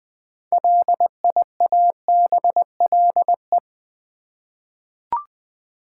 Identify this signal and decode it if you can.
Morse code, heard as a keyed tone, about 20 wpm, 700 Hz. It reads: LIABLE
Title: 6 Letter Words - First 3 Letters Given - 20wpm